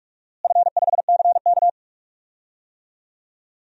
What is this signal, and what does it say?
Morse code, heard as a keyed tone, about 43 wpm, 700 Hz. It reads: F5CK
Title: DXCC-100 Call Signs - 4 Characters - Repeated 3x - 45wpm